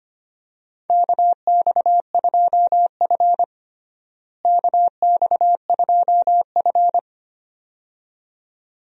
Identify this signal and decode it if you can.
Morse code, heard as a keyed tone, about 25 wpm, 700 Hz. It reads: KX2F KX2F
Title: USA Call Signs - 4 Characters - Repeated 2x - 25wpm